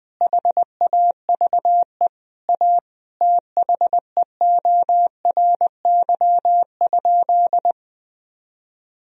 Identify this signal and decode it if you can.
Morse code, heard as a keyed tone, about 20 wpm, 700 Hz. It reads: HAVE A THEORY?
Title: Sentences from Top 1000 Words - Review Part 3 - 20wpm